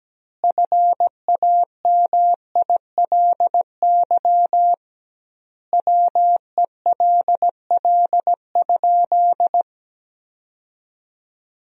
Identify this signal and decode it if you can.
Morse code, heard as a keyed tone, about 17 wpm, 700 Hz. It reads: FAMILY WELL?
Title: 2x Word-Spacing - Sentences from Top 400 Words - 17wpm